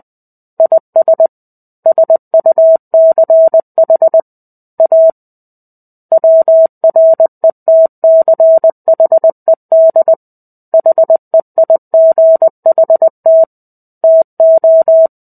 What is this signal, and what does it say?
Morse code, heard as a keyed tone, about 20 wpm, 650 Hz. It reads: IS SUCH A WRETCHED HEIGHT TO
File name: 05-Chapter-05 20 WPM